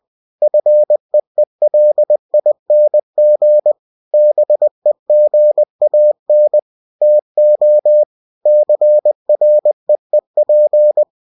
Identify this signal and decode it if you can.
Morse code, heard as a keyed tone, about 20 wpm, 600 Hz. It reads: FEELING BEGAN TO CREEP
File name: cw-books / winnie-the-pooh / winnie-20wpm-page-0049